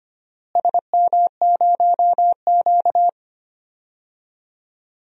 Morse code, heard as a keyed tone, about 25 wpm, 700 Hz. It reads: SM0Q
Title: DXCC-100 Call Signs - 4 Characters - Repeated 2x - 25wpm